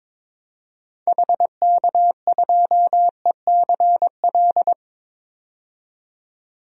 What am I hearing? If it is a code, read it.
Morse code, heard as a keyed tone, about 21 wpm, 700 Hz. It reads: HK2ECL
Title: DXCC-100 Call Signs - 6 Characters - 22wpm